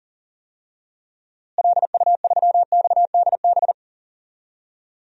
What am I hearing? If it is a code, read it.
Morse code, heard as a keyed tone, about 40 wpm, 700 Hz. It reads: LU3XDB
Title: DXCC-100 Call Signs - 6 Characters - Repeated 3x - 40wpm